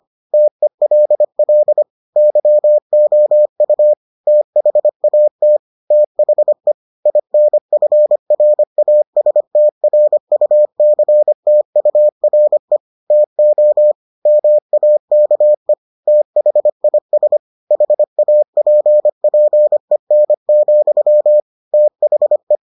Morse code, heard as a keyed tone, about 25 wpm, 600 Hz. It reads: TELL YOU THAT THE INFRASTRUCTURE TO MAKE THIS HAPPEN, THE